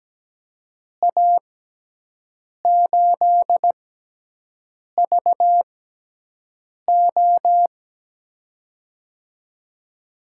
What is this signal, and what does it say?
Morse code, heard as a keyed tone, about 17 wpm, spaced as for 6 wpm, 700 Hz. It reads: A8VO
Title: USA Call Signs - 5 Characters - 6x Character Spacing - 17wpm